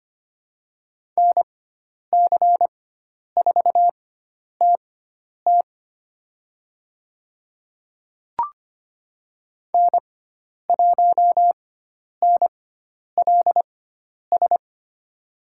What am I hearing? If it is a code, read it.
Morse code, heard as a keyed tone, about 25 wpm, spaced as for 10 wpm, 700 Hz. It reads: NC4TT N1NLS
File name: Lesson 16g - Letter c - Call Signs 25wpm